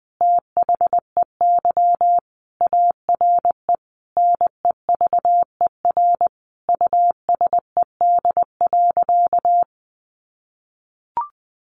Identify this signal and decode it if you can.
Morse code, heard as a keyed tone, about 19 wpm, 700 Hz. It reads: THEY ARE NEVER USED.
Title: Sentences from Top 600 Words 20wpm